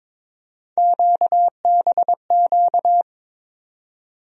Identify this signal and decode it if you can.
Morse code, heard as a keyed tone, about 23 wpm, 700 Hz. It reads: QBQ
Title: USA Call Sign Postfixes - 3 Letters 22wpm